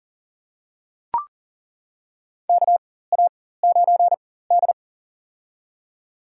Morse code, heard as a keyed tone, about 38 wpm, spaced as for 18 wpm, 700 Hz. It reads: KA9D
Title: USA Call Signs - 4 Characters - 4x Character Spacing - 40wpm